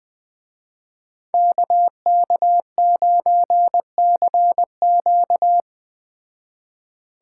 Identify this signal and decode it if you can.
Morse code, heard as a keyed tone, about 20 wpm, 700 Hz. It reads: KK9CQ
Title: Speed Racing - USA 2x2 Call Signs 17wpm